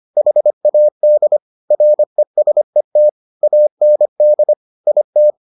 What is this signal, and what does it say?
Morse code, heard as a keyed tone, about 25 wpm, 600 Hz. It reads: HAD RESET AND IT